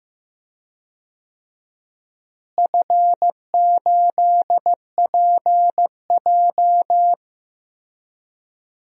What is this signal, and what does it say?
Morse code, heard as a keyed tone, about 15 wpm, 700 Hz. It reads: F8PJ
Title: DXCC-100 Call Signs - 4 Characters - Repeated 2x - 15wpm